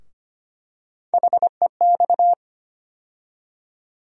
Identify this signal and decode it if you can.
Morse code, heard as a keyed tone, about 25 wpm, 700 Hz. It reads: HEX
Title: USA Call Sign Postfixes - 3 Letters 25wpm